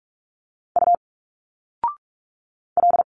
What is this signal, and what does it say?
Morse code, heard as a keyed tone, about 45 wpm, 700 Hz. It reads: U L
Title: Single Letters 45wpm